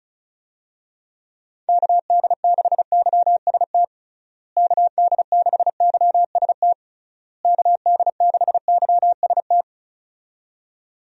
Morse code, heard as a keyed tone, about 34 wpm, 700 Hz. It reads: KD6YST KD6YST KD6YST
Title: USA Call Signs - 6 Characters - Repeated 3x - 35wpm